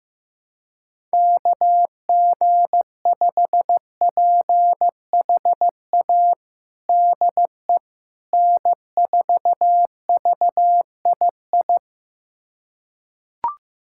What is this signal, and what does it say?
Morse code, heard as a keyed tone, about 16 wpm, 700 Hz. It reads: KG5PHA DE N4VII
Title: USA Callsign DE Callsign 15wpm